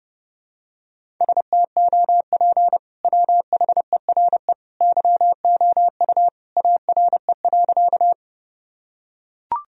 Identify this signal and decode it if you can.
Morse code, heard as a keyed tone, about 30 wpm, 700 Hz. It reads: STOP WHERE YOU ARE.
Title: Sentences from Top 1000 Words - Review Part 1 - 30wpm